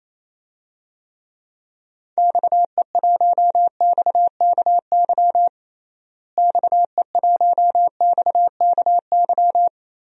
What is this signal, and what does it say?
Morse code, heard as a keyed tone, about 28 wpm, 700 Hz. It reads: XE1XKY XE1XKY
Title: DXCC-100 Call Signs - 6 Characters - Repeated 3x - 28wpm